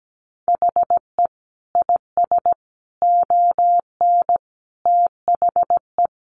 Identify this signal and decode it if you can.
Morse code, heard as a keyed tone, about 17 wpm, 700 Hz. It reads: HE IS ON THE
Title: Sentences from Top 1100 Words 17wpm